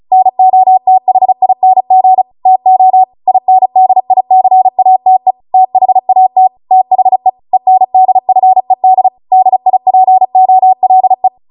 Morse code, heard as a keyed tone, about 36 wpm, 750 Hz. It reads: NOTHING TO INDICATE THAT THE ENDFED DIPOLE